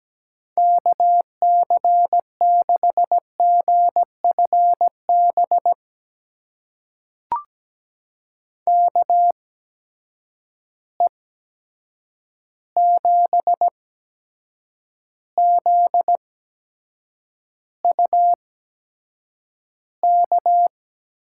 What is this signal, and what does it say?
Morse code, heard as a keyed tone, about 17 wpm, spaced as for 7 wpm, 700 Hz. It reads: KC6GFB KE7ZUK
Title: USA Call Signs - 6 Characters - 8x Character Spacing - 17wpm